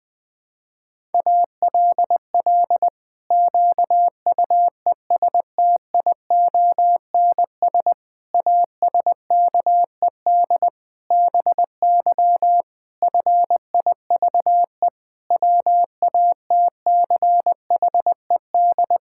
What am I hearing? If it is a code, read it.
Morse code, heard as a keyed tone, about 20 wpm, 700 Hz. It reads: ALL QUESTIONS ASKED BY FIVE WATCHED
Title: Pangram Challenge v2 - 20wpm